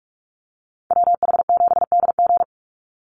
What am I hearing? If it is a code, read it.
Morse code, heard as a keyed tone, about 44 wpm, 700 Hz. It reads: WH7DG
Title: USA Call Signs 45wpm